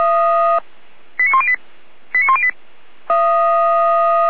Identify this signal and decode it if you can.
Five-tone selective call (ZVEI), heard as quick runs of five tones, tones of about 70 ms. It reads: 89108 89108